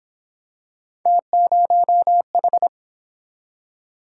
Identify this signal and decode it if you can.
Morse code, heard as a keyed tone, about 28 wpm, 700 Hz. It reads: T0H